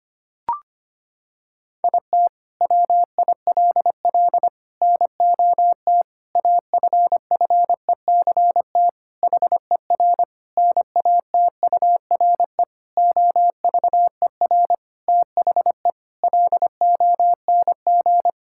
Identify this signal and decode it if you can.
Morse code, heard as a keyed tone, about 25 wpm, 700 Hz. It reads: IT WILL NOT AFFECT HER NATURE OVER THE LONG
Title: Sentences from Top 1000 Words - Review Part 6 - 25wpm